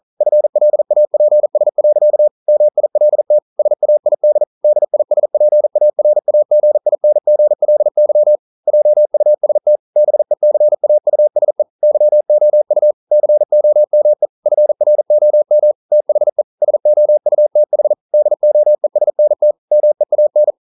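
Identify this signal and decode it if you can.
Morse code, heard as a keyed tone, about 39 wpm, 600 Hz. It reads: FLAPS. MILT SAID DISPARAGINGLY JUST BECAUSE YOU COME FROM THE SOUTH DOESNT MEAN
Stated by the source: Morse code practice at 41 wpm, from CW med Gnister.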